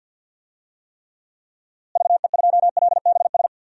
Morse code, heard as a keyed tone, about 52 wpm, 700 Hz. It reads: VE2LBS
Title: DXCC-100 Call Signs - 6 Characters - Repeated 2x - 50wpm